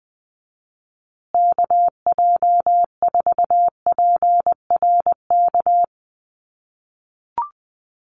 Morse code, heard as a keyed tone, about 19 wpm, 700 Hz. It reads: KJ4PRK
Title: USA Call Signs Repeated 3x 20wpm